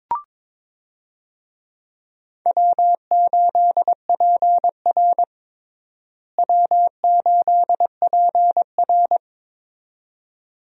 Morse code, heard as a keyed tone, about 22 wpm, 700 Hz. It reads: W8PR W8PR
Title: USA Call Signs - 4 Characters - Repeated 2x - 22wpm